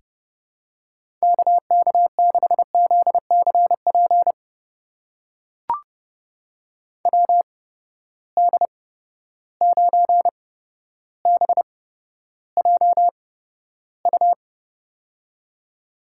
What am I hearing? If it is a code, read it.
Morse code, heard as a keyed tone, about 30 wpm, spaced as for 13 wpm, 700 Hz. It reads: KK6ZCP WD9BJU
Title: USA Call Signs - 8x Character Spacing - 30wpm